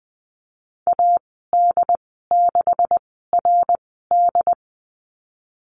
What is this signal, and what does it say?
Morse code, heard as a keyed tone, about 20 wpm, spaced as for 15 wpm, 700 Hz. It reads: AD6RD
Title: 154 Lesson 25g - Letter p - Call Signs 20wpm